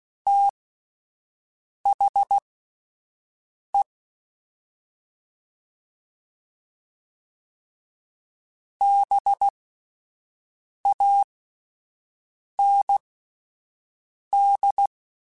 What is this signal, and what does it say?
Morse code, heard as a keyed tone, about 16 wpm, spaced as for 5 wpm, 800 Hz. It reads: THE BAND
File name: QSOs / QSOs-mp3s / WD0BGP 1@13